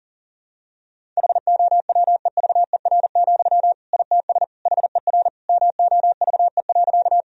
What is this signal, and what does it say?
Morse code, heard as a keyed tone, about 40 wpm, 700 Hz. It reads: HOWEVER, ITS HER MOVE.
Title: Sentences from Top 1000 Words - Review Part 1 - 40wpm